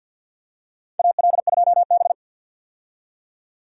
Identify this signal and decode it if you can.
Morse code, heard as a keyed tone, about 51 wpm, 700 Hz. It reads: AL2B
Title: USA Call Signs - 4 Characters - 50wpm